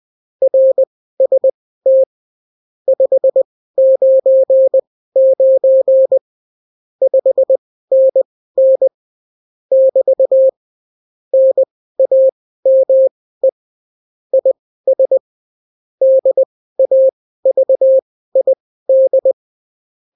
Morse code, heard as a keyed tone, about 20 wpm, spaced as for 14 wpm, 530 Hz. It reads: RST 599 5NN <BT> NAME IS DAVID
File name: CW QSO libroCW4 20-10ppm